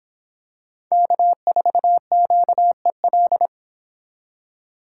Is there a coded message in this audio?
Yes: Morse code from 0.9 to 3.5 s, heard as a keyed tone, about 26 wpm, 700 Hz. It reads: K4QEL